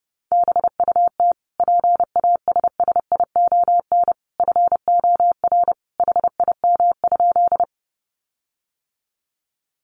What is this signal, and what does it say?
Morse code, heard as a keyed tone, about 30 wpm, 700 Hz. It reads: BUT PASSION FOR HIM?
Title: Sentences from Top 1300 Words 30wpm